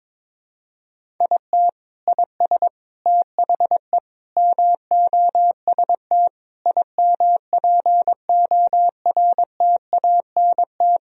Morse code, heard as a keyed tone, about 22 wpm, 700 Hz. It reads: IT IS THE MOST IMPORTANT
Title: Sentences from Top 1000 Words - Review Part 1 - 22wpm